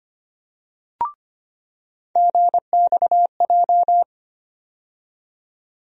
Morse code, heard as a keyed tone, about 25 wpm, 700 Hz. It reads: GXJ
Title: USA Call Sign Postfixes - 3 Letters 25wpm